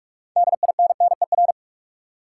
Morse code, heard as a keyed tone, about 47 wpm, 700 Hz. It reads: DINNER
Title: Six Letter Words - 45wpm